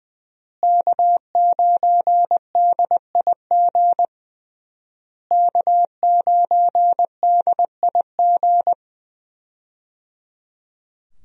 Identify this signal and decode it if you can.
Morse code, heard as a keyed tone, about 21 wpm, 700 Hz. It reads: K9DIG K9DIG
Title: USA Call Signs - 5 Characters - Repeated 3x - 20wpm